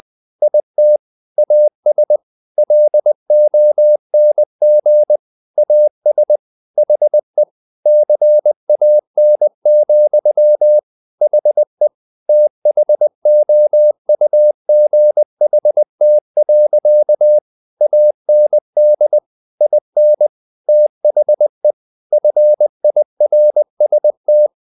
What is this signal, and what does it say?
Morse code, heard as a keyed tone, about 20 wpm, 600 Hz. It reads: IT AS LONG AS HE CAN, HE THOUGHT. AND IN THE FIRST
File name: the old man and the sea / the old man and the sea part 1 20wpm